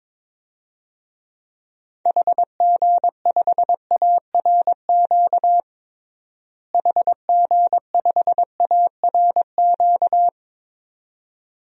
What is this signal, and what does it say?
Morse code, heard as a keyed tone, about 22 wpm, 700 Hz. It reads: HG5ARQ HG5ARQ
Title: DXCC-100 Call Signs - 6 Characters - Repeated 2x - 22wpm